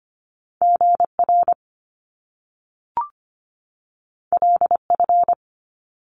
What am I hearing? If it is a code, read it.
Morse code, heard as a keyed tone, about 25 wpm, 700 Hz. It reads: GR LF
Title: Top Two Letter Combinations in Words 25wpm